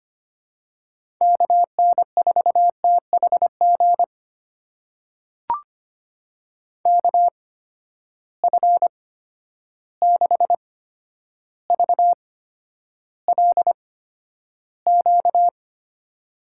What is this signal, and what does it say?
Morse code, heard as a keyed tone, about 25 wpm, spaced as for 11 wpm, 700 Hz. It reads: KN4THG KF6VLQ
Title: USA Call Signs - 6 Characters - 8x Character Spacing - 25wpm